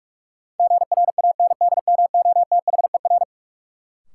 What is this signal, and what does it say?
Morse code, heard as a keyed tone, about 44 wpm, 700 Hz. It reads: GRANDMOTHER
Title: Sentences from Top 1200 Words - New Word Drill Alt - 45wpm